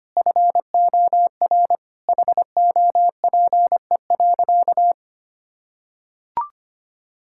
Morse code, heard as a keyed tone, about 25 wpm, 700 Hz. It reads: FOR HOPE.
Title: Sentences from Top 1000 Words - Review Part 5 - 25wpm